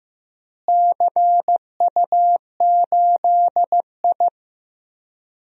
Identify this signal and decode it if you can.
Morse code, heard as a keyed tone, about 15 wpm, 700 Hz. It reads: CU8I
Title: DXCC-100 Call Signs - 4 Characters - Repeated 3x - 15wpm